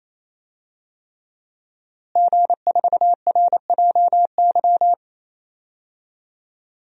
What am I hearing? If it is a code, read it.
Morse code, heard as a keyed tone, about 28 wpm, 700 Hz. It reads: G4RJY
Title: DXCC-100 Call Signs - 5 Characters - Repeated 3x - 28wpm